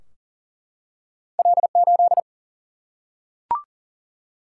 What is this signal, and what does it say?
Morse code, heard as a keyed tone, about 42 wpm, 700 Hz. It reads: L 8
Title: Single-Letter Single-Number 40wpm